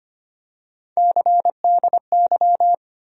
Morse code, heard as a keyed tone, about 25 wpm, 700 Hz. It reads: CDY
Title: USA Call Sign Postfixes - 3 Letters 25wpm